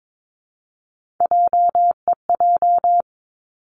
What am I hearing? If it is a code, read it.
Morse code, heard as a keyed tone, about 22 wpm, 700 Hz. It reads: JEJ
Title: Three Challenging Letter-Number Combinations 22wpm